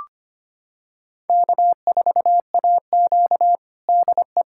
Morse code, heard as a keyed tone, about 25 wpm, 700 Hz. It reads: K4AQ DE
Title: USA Callsign DE Callsign - 4 Characters - 25wpm